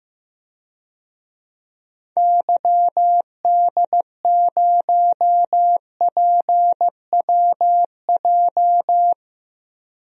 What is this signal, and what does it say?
Morse code, heard as a keyed tone, about 15 wpm, 700 Hz. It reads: YD0PWJ
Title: DXCC-100 Call Signs - 6 Characters - 15wpm